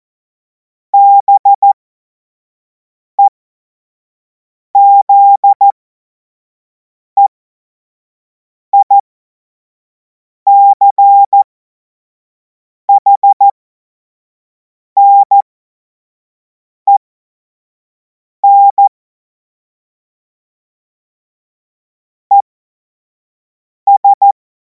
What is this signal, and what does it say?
Morse code, heard as a keyed tone, about 14 wpm, spaced as for 5 wpm, 800 Hz. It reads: BEZEICHNEN ES
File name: Morsekurs / Witze / Witz 24. Februar 2026 25 F70